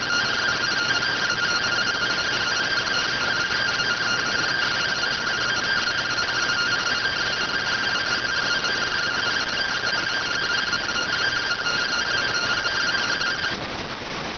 Two-tone teleprinter signal, about 170 Hz shift.